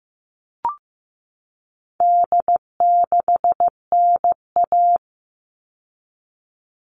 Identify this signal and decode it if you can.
Morse code, heard as a keyed tone, about 15 wpm, 700 Hz. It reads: D6NA
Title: All Cut Numbers - 4-Digit 15wpm